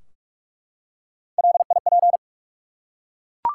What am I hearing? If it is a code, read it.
Morse code, heard as a keyed tone, about 44 wpm, 700 Hz. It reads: LIP
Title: Three Letter Words - 45wpm